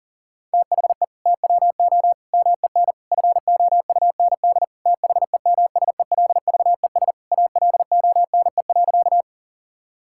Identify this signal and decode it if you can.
Morse code, heard as a keyed tone, about 40 wpm, 700 Hz. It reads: THE TWO MEN FOUND THEMSELVES ALONE.
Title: Sentences from Top 1000 Words - Review Part 5 - 40wpm